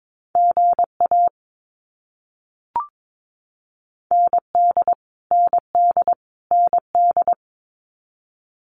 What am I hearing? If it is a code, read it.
Morse code, heard as a keyed tone, about 22 wpm, 700 Hz. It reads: GA ND ND ND
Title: US States Repeated 3x 22wpm